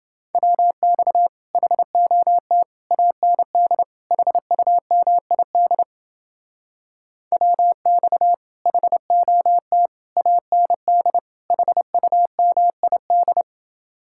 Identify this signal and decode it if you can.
Morse code, heard as a keyed tone, about 30 wpm, 700 Hz. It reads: WX HOT AND HUMID WX HOT AND HUMID